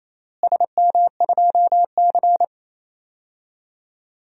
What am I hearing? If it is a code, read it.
Morse code, heard as a keyed tone, about 28 wpm, 700 Hz. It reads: SM2C
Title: DXCC-100 Call Signs - 4 Characters - Repeated 3x - 28wpm